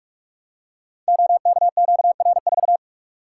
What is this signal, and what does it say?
Morse code, heard as a keyed tone, about 44 wpm, 700 Hz. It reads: OKQR4
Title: Code Groups - 5 Letter:Numbers - 45wpm